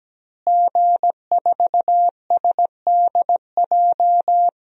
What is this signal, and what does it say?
Morse code, heard as a keyed tone, about 17 wpm, 700 Hz. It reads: G4SDJ
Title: DXCC-100 Call Signs - 5 Characters - Repeated 2x - 17wpm